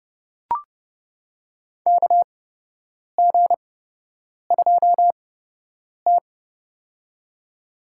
Morse code, heard as a keyed tone, about 31 wpm, spaced as for 8 wpm, 700 Hz. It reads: KG2T
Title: USA Call Signs - 4 Characters - 8x Character Spacing - 30wpm